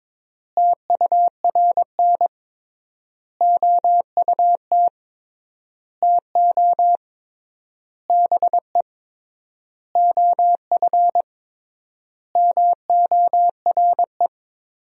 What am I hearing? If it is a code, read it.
Morse code, heard as a keyed tone, about 22 wpm, 700 Hz. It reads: TURN OUT TO BE OF MORE
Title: 3x Word-Spacing - Sentences from Top 400 Words - 22wpm